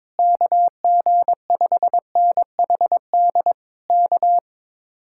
Morse code, heard as a keyed tone, about 21 wpm, 700 Hz. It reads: KG5NHD K
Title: Speed-Racing Common QSO Phrases Encore - 22wpm